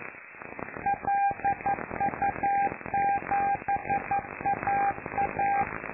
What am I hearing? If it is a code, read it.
Morse code, heard as a keyed tone, about 11 wpm, 800 Hz, with noise. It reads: LU7AA
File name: lu4aao / Globo Amsat